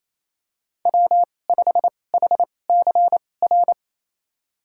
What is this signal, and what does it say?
Morse code, heard as a keyed tone, about 28 wpm, spaced as for 20 wpm, 700 Hz. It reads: W5HCR
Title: USA Call Signs - 5 Characters - 2x Character Spacing - 28wpm